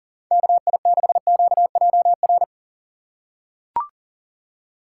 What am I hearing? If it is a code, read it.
Morse code, heard as a keyed tone, about 40 wpm, 700 Hz. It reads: KI6QJR
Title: USA Call Signs - 6 Characters - 3x Character Spacing - 40wpm